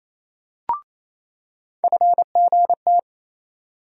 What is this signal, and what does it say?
Morse code, heard as a keyed tone, about 28 wpm, 700 Hz. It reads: FGT
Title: USA Call Sign Postfixes - 3 Letters 28wpm